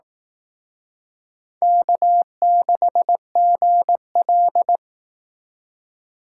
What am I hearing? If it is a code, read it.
Morse code, heard as a keyed tone, about 18 wpm, 700 Hz. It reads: K6GL
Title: Speed Racing - USA 1x2 Call Signs 15wpm